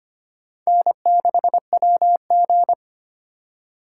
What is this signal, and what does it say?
Morse code, heard as a keyed tone, about 25 wpm, 700 Hz. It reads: N6WG
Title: USA Call Signs - 4 Characters - 2x Character Spacing - 25wpm